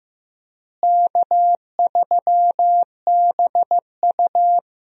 Morse code, heard as a keyed tone, about 15 wpm, 700 Hz. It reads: K3BU
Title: USA Call Signs - 4 Characters - Repeated 3x - 15wpm